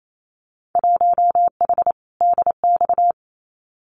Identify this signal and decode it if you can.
Morse code, heard as a keyed tone, about 28 wpm, 700 Hz. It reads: 1H DX
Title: Winter Field Day Exchanges 28wpm